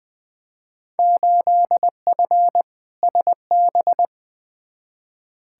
Morse code, heard as a keyed tone, about 20 wpm, 700 Hz. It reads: 8F SB
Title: Field Day Exchanges Encore - 20wpm